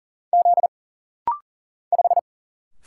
Morse code, heard as a keyed tone, about 40 wpm, 700 Hz. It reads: Z 5
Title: Single Letter-Number Alt - Rapid-Fire 40wpm